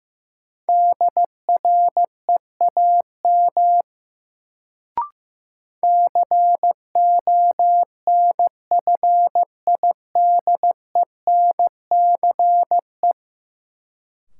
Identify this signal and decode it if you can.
Morse code, heard as a keyed tone, about 15 wpm, 700 Hz. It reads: DREAM CONFIDENCE
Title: Sentences from Top 1500 Words - New Word Drill - 15wpm